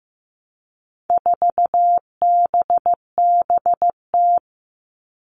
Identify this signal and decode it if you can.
Morse code, heard as a keyed tone, about 15 wpm, 700 Hz. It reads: 4BBT
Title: All Cut Numbers - 4-Digit 15wpm